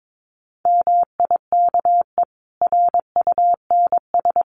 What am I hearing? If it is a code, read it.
Morse code, heard as a keyed tone, about 22 wpm, 700 Hz. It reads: MIKE RUNS
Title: Additional Easy Phrases 22wpm